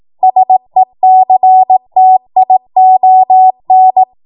Morse code, heard as a keyed tone, about 18 wpm, 750 Hz. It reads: SECTION
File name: Morse / 18WPM